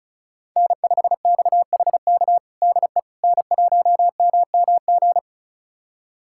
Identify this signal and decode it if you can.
Morse code, heard as a keyed tone, about 36 wpm, 700 Hz. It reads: N5XHK DE N1MMG
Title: USA Callsign DE Callsign - 5 Characters - 35wpm